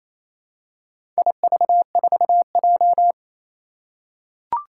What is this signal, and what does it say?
Morse code, heard as a keyed tone, about 29 wpm, 700 Hz. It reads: IV4J
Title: Four Challenging Letter-Number Combinations 28wpm